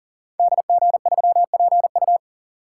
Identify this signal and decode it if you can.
Morse code, heard as a keyed tone, about 42 wpm, 700 Hz. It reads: DG3PU
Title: Code Groups - 5 Letter:Numbers - 40wpm